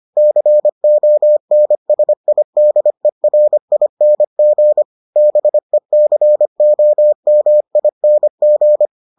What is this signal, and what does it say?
Morse code, heard as a keyed tone, about 25 wpm, 600 Hz. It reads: CONSIDERING BECOMING